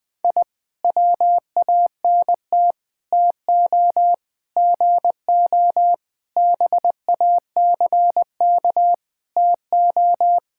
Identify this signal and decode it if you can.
Morse code, heard as a keyed tone, about 20 wpm, 700 Hz. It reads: I WANT TO GO BACK TO
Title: Sentences with 4-Word Phrases - Spoken Beginning - 20wpm